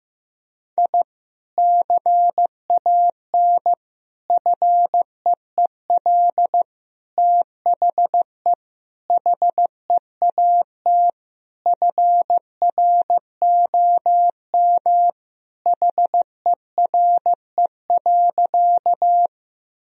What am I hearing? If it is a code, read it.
Morse code, heard as a keyed tone, about 15 wpm, 700 Hz. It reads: I CAN FEEL THE HEAT FROM HERE.
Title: Sentences from Top 1000 Words - Review Part 3 - 15wpm